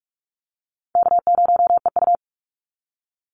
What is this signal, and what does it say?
Morse code, heard as a keyed tone, about 44 wpm, 700 Hz. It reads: K0EU
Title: Easy Call Signs Repeated 2x - 45wpm